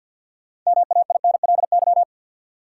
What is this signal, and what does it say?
Morse code, heard as a keyed tone, about 49 wpm, 700 Hz. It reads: MAINLY
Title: Sentences from Top 1400 Words - New Word Drill - 50wpm